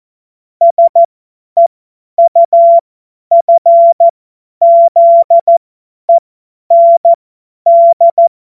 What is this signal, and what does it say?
Morse code, heard as a keyed tone, about 14 wpm, spaced as for 10 wpm, 680 Hz. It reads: SEUFZEND
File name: Morsekurs / Witze / Witz 05 März 2024 50 F70